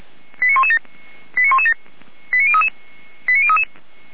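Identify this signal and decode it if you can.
Five-tone selective call (ZVEI), heard as quick runs of five tones, tones of about 70 ms. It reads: 89117 89117 89033 89033
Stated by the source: Five-tone ZVEI selective call received by SDR.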